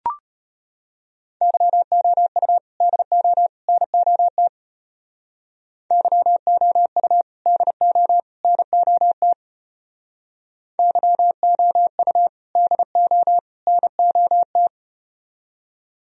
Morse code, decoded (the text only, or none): YOU DO NOT YOU DO NOT YOU DO NOT